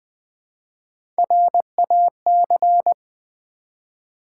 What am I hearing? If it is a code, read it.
Morse code, heard as a keyed tone, about 20 wpm, 700 Hz. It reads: RAC
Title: Build A Word - 1 Letter At A Time - 3 to 5-Letter Words - 20wpm